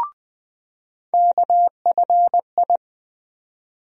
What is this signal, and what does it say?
Morse code, heard as a keyed tone, about 20 wpm, 700 Hz. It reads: KFI
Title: USA Call Sign Postfixes - 3 Letters 20wpm